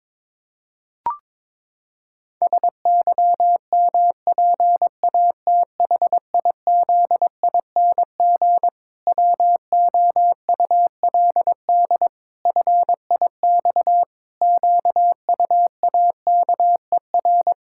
Morse code, heard as a keyed tone, about 22 wpm, 700 Hz. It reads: SYMPATHIZING WOULD FIX QUAKER
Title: Pangram Challenge v2 - 22wpm